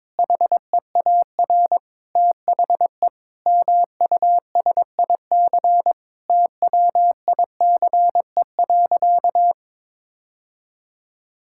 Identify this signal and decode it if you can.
Morse code, heard as a keyed tone, about 22 wpm, 700 Hz. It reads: HEAR THE MUSIC TWICE.
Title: Sentences from Top 1600 Words 22wpm